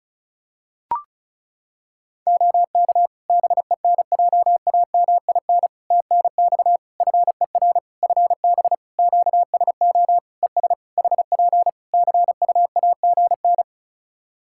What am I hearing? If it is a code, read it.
Morse code, heard as a keyed tone, about 36 wpm, 700 Hz. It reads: OK BENJAMIN TNX FER FB QSO ES HP CUAGN
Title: Practice QSO - Exchange 4 - Ending - 35wpm